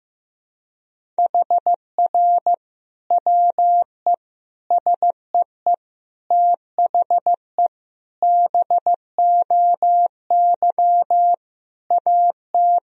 Morse code, heard as a keyed tone, about 15 wpm, 700 Hz. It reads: HR WE SEE THE BOY AT
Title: Sentences from Top 300 Words and Abbreviations 15wpm